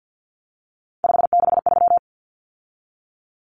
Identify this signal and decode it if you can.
Morse code, heard as a keyed tone, about 50 wpm, 700 Hz. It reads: 563
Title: Numbers - 3-Digit 50wpm